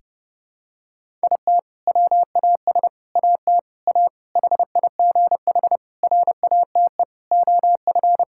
Morse code, heard as a keyed tone, about 32 wpm, 700 Hz. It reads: IT WAS AT A HIGH RATE OF
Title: Sentences from Top 1000 Words - Review Part 5 - 30wpm